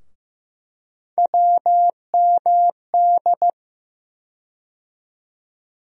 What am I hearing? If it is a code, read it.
Morse code, heard as a keyed tone, about 15 wpm, 700 Hz. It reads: WMD
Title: USA Call Sign Postfixes - 3 Letters 15wpm